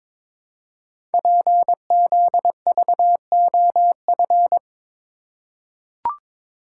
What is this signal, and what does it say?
Morse code, heard as a keyed tone, about 22 wpm, 700 Hz. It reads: PZVOF
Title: Code Groups - 5 Letters - 22wpm